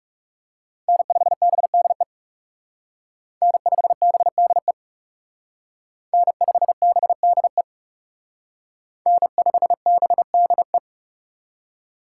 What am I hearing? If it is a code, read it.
Morse code, heard as a keyed tone, about 34 wpm, 700 Hz. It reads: N5BDE N5BDE N5BDE N5BDE
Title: Speed Racing - USA 1x3 Call Signs 30wpm